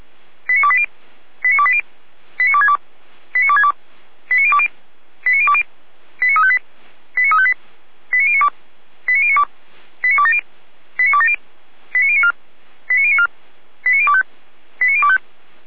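Five-tone selective call (ZVEI), heard as quick runs of five tones, tones of about 70 ms. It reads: 89280 89280 89262 89262 89020 89020 89358 89358 89092 89092 89270 89270 89094 89094 89025 89025
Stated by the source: ZVEI five-tone selective call received on SDR.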